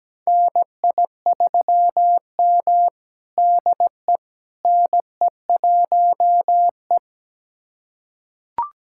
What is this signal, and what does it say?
Morse code, heard as a keyed tone, about 17 wpm, 700 Hz. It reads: NI3M DE NE1E
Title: USA Callsign DE Callsign - 4 Characters - 17wpm